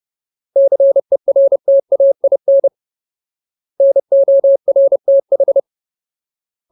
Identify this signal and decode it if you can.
Morse code, heard as a keyed tone, about 30 wpm, 550 Hz. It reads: CERTAIN NORTH
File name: Ultimate Word List / 30WPM Ultimate